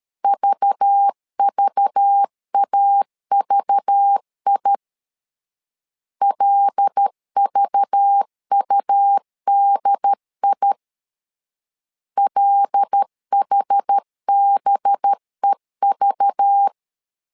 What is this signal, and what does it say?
Morse code, heard as a keyed tone, about 13 wpm, 800 Hz. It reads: VVAVI LVUDI LHBEV
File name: LEZIONI / 5 LBVC / 5 LBV 12wpm 1855